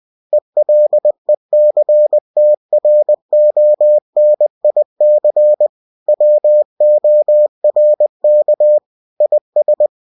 Morse code, heard as a keyed tone, about 20 wpm, 600 Hz. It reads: ELECTRONIC WORK IS
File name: cw-books / winnie-the-pooh / winnie-20wpm-page-0117